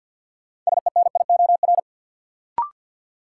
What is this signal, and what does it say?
Morse code, heard as a keyed tone, about 53 wpm, 700 Hz. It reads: SENIOR
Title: Sets of 5 Words Encore - New Word Drill - 50wpm